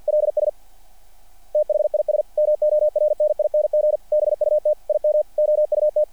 Morse code, heard as a keyed tone, about 50 wpm, 600 Hz. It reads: LS THIS MORNING BUT IM OUT
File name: archived / CRSnet